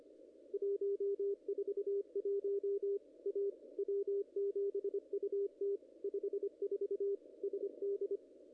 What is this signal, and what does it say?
Morse code, heard as a keyed tone, about 25 wpm, 400 Hz, with radio noise. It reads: 141 A W7UT 54 SD
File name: contests / ss-cw-k5tr-op-n5rz -0000